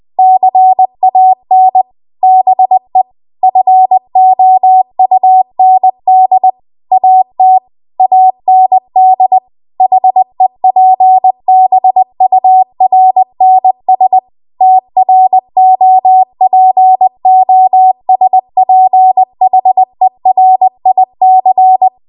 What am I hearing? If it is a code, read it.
Morse code, heard as a keyed tone, about 20 wpm, 750 Hz. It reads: CAN BE FOUND AT AND HEPBURNS TROPOSPHERIC
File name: Archive / 20 WPM / 20WPM